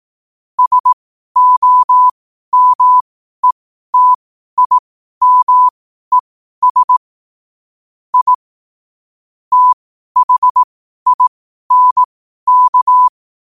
Morse code, heard as a keyed tone, about 18 wpm, spaced as for 12 wpm, 1000 Hz. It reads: SOMETIMES I THINK
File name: qotdfiles / 12wpm